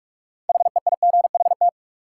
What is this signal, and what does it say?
Morse code, heard as a keyed tone, about 45 wpm, 700 Hz. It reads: HEIGHT
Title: Sentences from Top 1600 Words - New Word Drill Alt - 45wpm